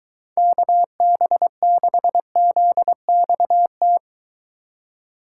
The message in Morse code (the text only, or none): KB6ZXT